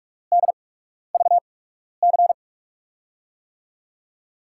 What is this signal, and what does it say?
Morse code, heard as a keyed tone, about 44 wpm, spaced as for 12 wpm, 700 Hz. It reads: DVC
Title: USA Call Signs - 5 Characters - 8x Character Spacing - 45wpm